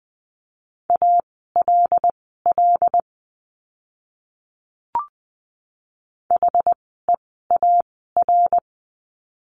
Morse code, heard as a keyed tone, about 20 wpm, spaced as for 15 wpm, 700 Hz. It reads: ALL HEAR
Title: 206 Lesson 33f - Letter j - Words 20wpm